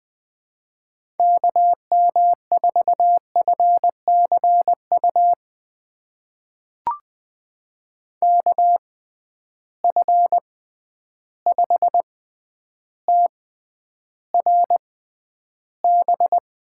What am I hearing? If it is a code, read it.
Morse code, heard as a keyed tone, about 20 wpm, spaced as for 10 wpm, 700 Hz. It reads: KM4FCU KF5TRB
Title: USA Call Signs - 6 Characters - 6x Character Spacing - 20wpm